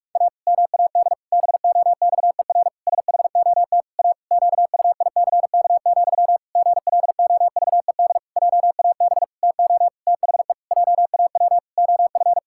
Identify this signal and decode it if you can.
Morse code, heard as a keyed tone, about 46 wpm, 700 Hz. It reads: A MAD BOXER SHOT A QUICK, GLOVED JAB TO THE JAW OF
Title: Pangram Challenge v2 - 45wpm